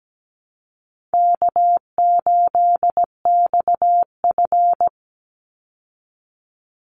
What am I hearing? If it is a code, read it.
Morse code, heard as a keyed tone, about 17 wpm, 700 Hz. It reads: K8XF
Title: Call Signs Repeated 3x - 17wpm